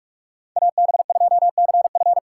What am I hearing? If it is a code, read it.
Morse code, heard as a keyed tone, about 46 wpm, 700 Hz. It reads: AB2CF
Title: USA Call Signs - 5 Characters - 3x Character Spacing - 45wpm